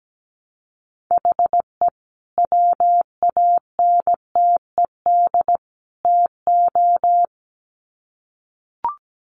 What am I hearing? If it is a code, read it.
Morse code, heard as a keyed tone, about 17 wpm, 700 Hz. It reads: HE WANTED TO
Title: Sets of 3 Words 17wpm